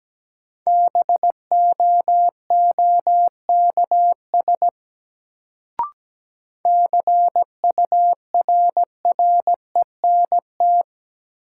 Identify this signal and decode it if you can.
Morse code, heard as a keyed tone, about 17 wpm, 700 Hz. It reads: BOOKS CURRENT
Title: Sentences from Top 1000 Words - Review Part 4 - New Word Drill - 17wpm